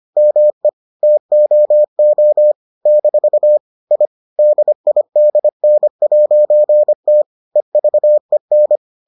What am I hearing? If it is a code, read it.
Morse code, heard as a keyed tone, about 25 wpm, 600 Hz. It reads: ME TOO - I DIDN'T EVEN